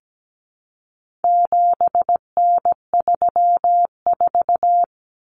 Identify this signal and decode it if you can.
Morse code, heard as a keyed tone, about 17 wpm, 700 Hz. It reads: 7N34
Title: Common Cut Numbers - 4-Digit 17wpm